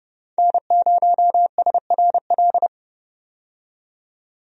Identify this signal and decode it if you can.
Morse code, heard as a keyed tone, about 32 wpm, 700 Hz. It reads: N0SRL
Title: USA Call Signs - 5 Characters - 30wpm